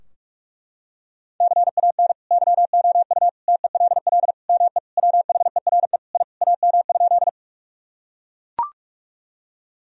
Morse code, heard as a keyed tone, about 41 wpm, 700 Hz. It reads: CAN YOU TELL ME WHERE I AM?
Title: Sentences from Top 1000 Words - Review Part 2 - 45wpm